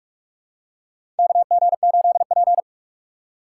Morse code, heard as a keyed tone, about 44 wpm, 700 Hz. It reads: KG8P
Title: USA Call Signs - 6x Character Spacing - 45wpm